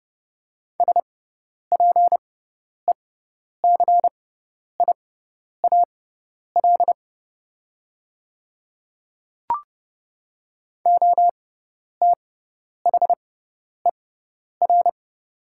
Morse code, heard as a keyed tone, about 30 wpm, spaced as for 10 wpm, 690 Hz. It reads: SPECIAL OTHER
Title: 6x Character Spacing - Top 500 Words - 30wpm